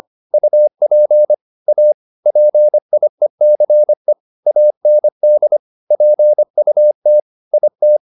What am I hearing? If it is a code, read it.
Morse code, heard as a keyed tone, about 25 wpm, 600 Hz. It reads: UP A PIECE AND PUT IT
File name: the old man and the sea part 1 25wpm